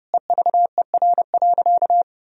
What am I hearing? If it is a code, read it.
Morse code, heard as a keyed tone, about 29 wpm, 700 Hz. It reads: EVER.
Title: Sentences from Top 1000 Words - Review Part 4 - 30wpm